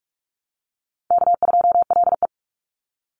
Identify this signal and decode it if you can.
Morse code, heard as a keyed tone, about 44 wpm, 700 Hz. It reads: K2LE
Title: Easy Call Signs - 45wpm